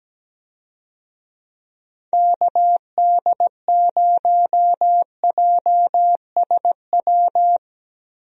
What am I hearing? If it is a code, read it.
Morse code, heard as a keyed tone, about 17 wpm, 700 Hz. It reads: KD0JSW
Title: USA Call Signs - 6 Characters - Repeated 3x - 17wpm